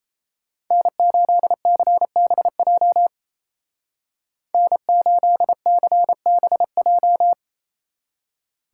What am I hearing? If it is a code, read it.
Morse code, heard as a keyed tone, about 34 wpm, 700 Hz. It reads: N8CBJ N8CBJ
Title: Speed Racing - USA 1x3 Call Signs 28wpm